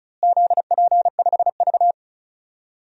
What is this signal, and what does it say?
Morse code, heard as a keyed tone, about 36 wpm, 700 Hz. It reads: ZP5V
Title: DXCC-100 Call Signs - 4 Characters - Repeated 3x - 35wpm